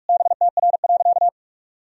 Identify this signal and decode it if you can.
Morse code, heard as a keyed tone, about 46 wpm, 700 Hz. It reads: BTR.
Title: Sentences from Top 500 Words and Abbreviations 45wpm